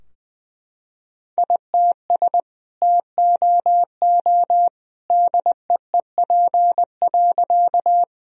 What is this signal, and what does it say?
Morse code, heard as a keyed tone, about 20 wpm, 700 Hz. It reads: ITS TOO DEEP.
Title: Sentences from Top 1000 Words - Review Part 3 - 20wpm